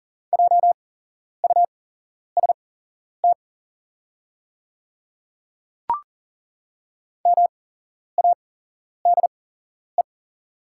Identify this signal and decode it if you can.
Morse code, heard as a keyed tone, about 40 wpm, spaced as for 11 wpm, 690 Hz. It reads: JUST MADE
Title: 8x Character Spacing - Top 300 Words - 40wpm